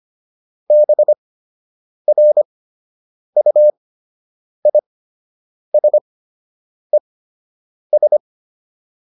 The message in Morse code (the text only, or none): BRUISES